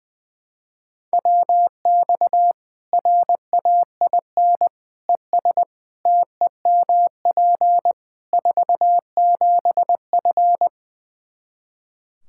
Morse code, heard as a keyed tone, about 20 wpm, 700 Hz. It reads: WX RAIN ES TEMP 47F
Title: Practice QSO - Exchange 2 - Info - 20wpm